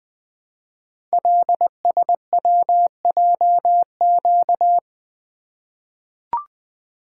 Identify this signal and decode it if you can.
Morse code, heard as a keyed tone, about 20 wpm, 700 Hz. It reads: LSWJQ
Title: Code Groups - 5 Letter:Numbers - 20wpm